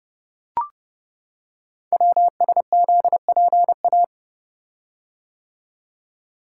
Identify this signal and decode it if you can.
Morse code, heard as a keyed tone, about 30 wpm, 700 Hz. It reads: WSZPA
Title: Code Groups - 5 Letters - 30wpm